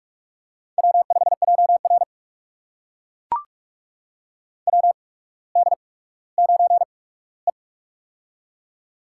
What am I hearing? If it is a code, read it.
Morse code, heard as a keyed tone, about 44 wpm, spaced as for 19 wpm, 700 Hz. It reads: W5JR WD9E
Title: USA Call Signs - 4 Characters - 8x Character Spacing - 45wpm